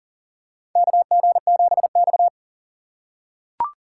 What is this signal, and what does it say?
Morse code, heard as a keyed tone, about 40 wpm, 700 Hz. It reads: KG7X
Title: USA Call Signs - 4 Characters - 2x Character Spacing - 40wpm